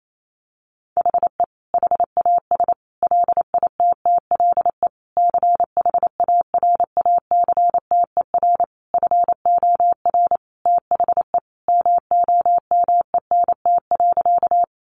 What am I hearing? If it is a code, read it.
Morse code, heard as a keyed tone, about 28 wpm, 700 Hz. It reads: HE HAS LITTLE CHARACTER FOR THE MOMENT.
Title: Sentences from Top 1200 Words 28wpm